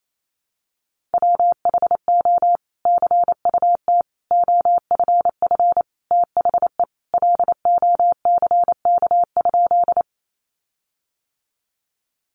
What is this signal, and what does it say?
Morse code, heard as a keyed tone, about 28 wpm, 700 Hz. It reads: WHO CUT OFF THE LOCK?
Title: Sentences from Top 1300 Words 28wpm